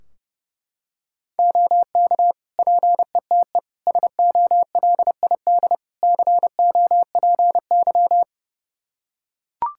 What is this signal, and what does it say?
Morse code, heard as a keyed tone, about 30 wpm, 700 Hz. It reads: OK PETE SOLID COPY
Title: Practice QSO - Exchange 3 - More Info - 30wpm